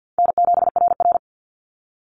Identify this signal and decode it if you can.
Morse code, heard as a keyed tone, about 50 wpm, 700 Hz. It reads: N7RR
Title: NA SOTA Activator Callsigns 50wpm